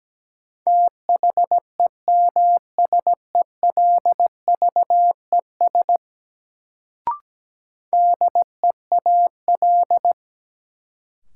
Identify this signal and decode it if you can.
Morse code, heard as a keyed tone, about 17 wpm, 700 Hz. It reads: THEMSELVES DEAL
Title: Sentences from Top 800 Words - New Word Drill - 17wpm